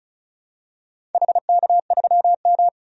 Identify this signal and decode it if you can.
Morse code, heard as a keyed tone, about 34 wpm, 700 Hz. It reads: HK3M
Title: DXCC-100 Call Signs - 4 Characters - 35wpm